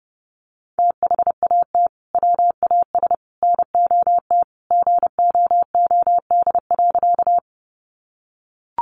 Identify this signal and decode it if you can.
Morse code, heard as a keyed tone, about 28 wpm, 700 Hz. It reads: THAT WAS NOT GOOD.
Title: Sentences from Top 600 Words 30wpm